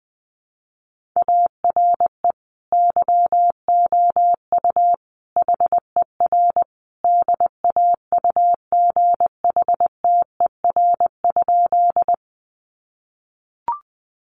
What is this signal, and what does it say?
Morse code, heard as a keyed tone, about 20 wpm, 700 Hz. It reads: ARE YOU HER DAUGHTER?
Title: Sentences from Top 1500 Words 20wpm